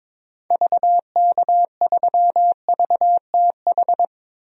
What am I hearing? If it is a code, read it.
Morse code, heard as a keyed tone, about 21 wpm, 700 Hz. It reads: VK3VTH
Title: DXCC-100 Call Signs - 6 Characters - 22wpm